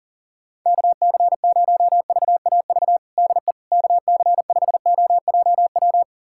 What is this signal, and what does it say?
Morse code, heard as a keyed tone, about 40 wpm, 700 Hz. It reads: KC0VAV DE KC5OJW
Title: USA Callsign DE Callsign 40wpm